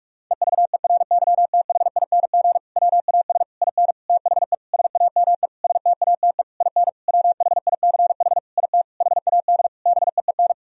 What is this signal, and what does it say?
Morse code, heard as a keyed tone, about 46 wpm, 700 Hz. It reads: EVERYTHING WAS IN THE SAME STATE IN WHICH IT HAD BEEN
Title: Sentences from Top 1000 Words - Review Part 2 - 45wpm